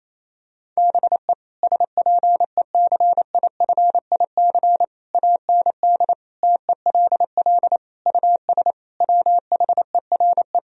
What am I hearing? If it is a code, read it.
Morse code, heard as a keyed tone, about 27 wpm, 700 Hz. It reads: BE SPECIFIC AND TELL US WHERE
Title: Sentences from Top 1000 Words - Review Part 3 - 28wpm